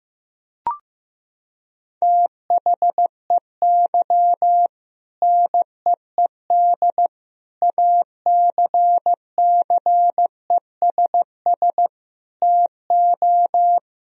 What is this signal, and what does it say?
Morse code, heard as a keyed tone, about 15 wpm, 700 Hz. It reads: THEY NEED ACCESS TO
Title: Sentences from Top 1000 Words - Review Part 4 - 15wpm